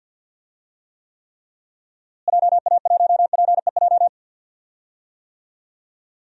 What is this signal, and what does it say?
Morse code, heard as a keyed tone, about 48 wpm, 700 Hz. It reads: JA1PEJ
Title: DXCC-100 Call Signs - 6 Characters - Repeated 3x - 50wpm